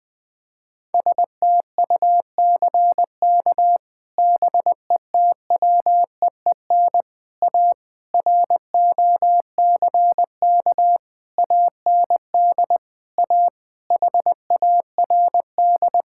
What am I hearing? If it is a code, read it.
Morse code, heard as a keyed tone, about 20 wpm, 700 Hz. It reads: STUCK BETWEEN A ROCK AND A HARD
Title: Sets of 8 Words - 20wpm